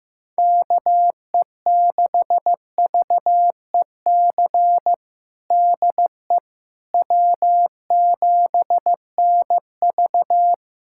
Morse code, heard as a keyed tone, about 15 wpm, 700 Hz. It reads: KE6VEC DE W7NV
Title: USA Callsign DE Callsign 15wpm